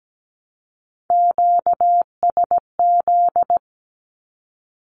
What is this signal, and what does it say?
Morse code, heard as a keyed tone, about 17 wpm, 700 Hz. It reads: QSZ
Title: Q-Codes 17wpm